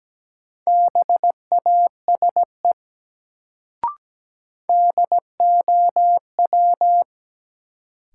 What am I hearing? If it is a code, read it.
Morse code, heard as a keyed tone, about 17 wpm, 700 Hz. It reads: BASE DOW
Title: Sets of 5 Words - New Word Drill - 17wpm